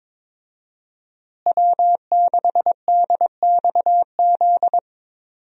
Morse code, heard as a keyed tone, about 22 wpm, 700 Hz. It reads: W6DXZ
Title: USA Call Signs - 5 Characters - Repeated 2x - 22wpm